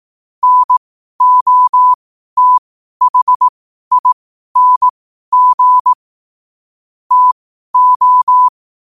Morse code, heard as a keyed tone, about 18 wpm, spaced as for 12 wpm, 1000 Hz. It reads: NOTHING TO
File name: qotdfiles / 12wpm